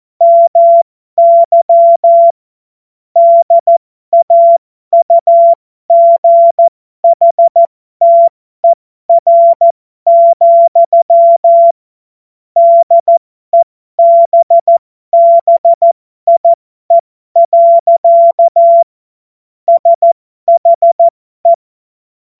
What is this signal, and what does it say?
Morse code, heard as a keyed tone, about 14 wpm, spaced as for 12 wpm, 680 Hz. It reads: MY DAUGHTER, DEBBIE. SHE
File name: Morsekurs / Witze